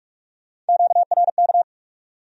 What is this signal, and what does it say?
Morse code, heard as a keyed tone, about 66 wpm, spaced as for 55 wpm, 700 Hz. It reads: QRK
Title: Speed-Racing Q-Codes - 45wpm